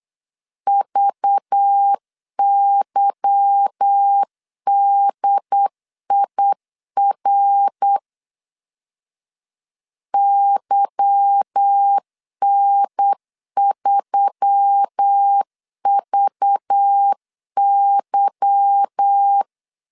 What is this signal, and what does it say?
Morse code, heard as a keyed tone, about 8 wpm, 800 Hz. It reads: VYDIR YN3VY
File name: LEZIONI / 8 YJ45 / 8 Y 8wpm 3245